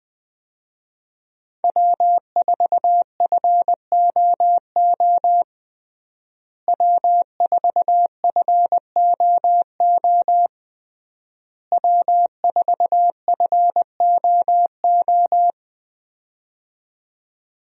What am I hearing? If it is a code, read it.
Morse code, heard as a keyed tone, about 20 wpm, 700 Hz. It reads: W4FOO W4FOO W4FOO
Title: USA Call Signs - 5 Characters - Repeated 3x - 20wpm